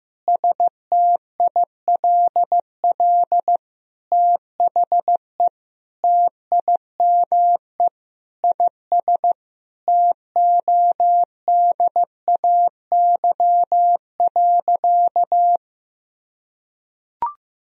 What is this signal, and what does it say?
Morse code, heard as a keyed tone, about 15 wpm, 700 Hz. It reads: STILL THE TIME IS TODAY.
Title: Sentences from Top 1000 Words - Review Part 2 - 15wpm